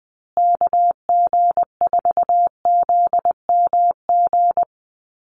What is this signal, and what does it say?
Morse code, heard as a keyed tone, about 20 wpm, 700 Hz. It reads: KG4ZMG
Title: USA Call Signs Repeated 2x 20wpm